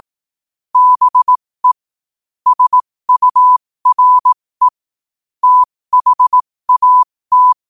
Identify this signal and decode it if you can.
Morse code, heard as a keyed tone, about 18 wpm, spaced as for 15 wpm, 1000 Hz. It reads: BE SURE THAT
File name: qotdfiles / 15wpm